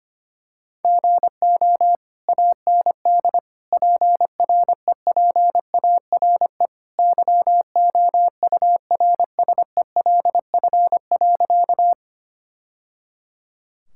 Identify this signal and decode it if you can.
Morse code, heard as a keyed tone, about 25 wpm, 700 Hz. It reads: GO AND PREPARE YOURSELF.
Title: Sentences from Top 1000 Words - Review Part 6 - 25wpm